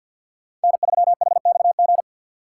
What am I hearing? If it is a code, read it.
Morse code, heard as a keyed tone, about 51 wpm, 700 Hz. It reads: N3HXG
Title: USA Call Signs - 5 Characters - 2x Character Spacing - 50wpm